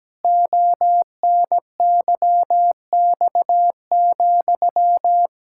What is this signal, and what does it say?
Morse code, heard as a keyed tone, about 17 wpm, 700 Hz. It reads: ONYX,
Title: Pangram Challenge v2 - 17wpm